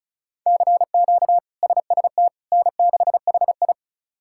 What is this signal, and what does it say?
Morse code, heard as a keyed tone, about 34 wpm, 700 Hz. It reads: CQ SST N6HI
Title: K1USN SST QSO Practice - 35wpm